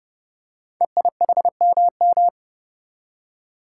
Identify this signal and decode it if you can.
Morse code, heard as a keyed tone, about 30 wpm, 700 Hz. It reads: EIHMM
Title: Code Groups - 5 Letters - 30wpm